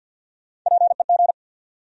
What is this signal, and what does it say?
Morse code, heard as a keyed tone, about 49 wpm, 700 Hz. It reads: PEG
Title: USA Common Names - Max 3 Letters - 50wpm